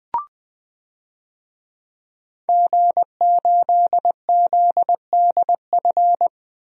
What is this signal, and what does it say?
Morse code, heard as a keyed tone, about 20 wpm, 700 Hz. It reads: G8ZDF
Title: DXCC-100 Call Signs - 5 Characters - 20wpm